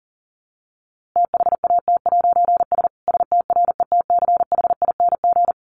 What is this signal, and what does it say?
Morse code, heard as a keyed tone, about 40 wpm, 700 Hz. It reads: THAT'S STRETCHING
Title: A Sound Of Thunder 40wpm